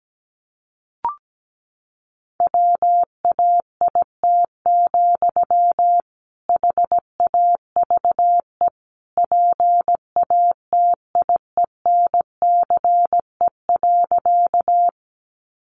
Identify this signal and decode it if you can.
Morse code, heard as a keyed tone, about 17 wpm, 700 Hz. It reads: WAIT, HAVE PATIENCE.
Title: Sentences from Top 1300 Words 17wpm